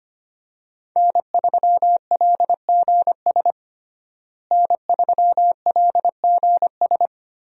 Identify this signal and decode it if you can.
Morse code, heard as a keyed tone, about 25 wpm, 700 Hz. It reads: N3LGS N3LGS
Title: USA Call Signs - 5 Characters - Repeated 3x - 25wpm